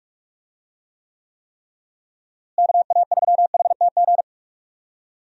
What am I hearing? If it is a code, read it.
Morse code, heard as a keyed tone, about 46 wpm, 700 Hz. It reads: KA3HTG
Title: USA Call Signs - 6 Characters - Repeated 3x - 45wpm